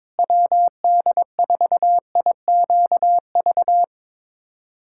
Morse code, heard as a keyed tone, about 22 wpm, 700 Hz. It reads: WD4IQV
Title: USA Callsign DE Callsign - 6 Characters - 22wpm